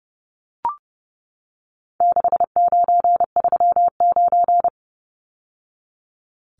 Morse code, heard as a keyed tone, about 30 wpm, 700 Hz. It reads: 6939
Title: Numbers - 4-Digit 30wpm